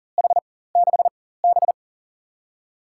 Morse code, heard as a keyed tone, about 40 wpm, 700 Hz. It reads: H 6 B
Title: USA Call Signs - 4x Character Spacing - 40wpm